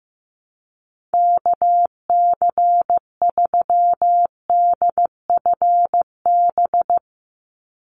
Morse code, heard as a keyed tone, about 15 wpm, 700 Hz. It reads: KC3DFB
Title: USA Call Signs 15wpm